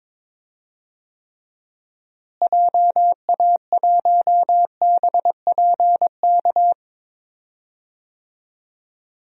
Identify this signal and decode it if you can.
Morse code, heard as a keyed tone, about 22 wpm, 700 Hz. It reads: JA1BPK
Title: DXCC-100 Call Signs - 6 Characters - Repeated 3x - 22wpm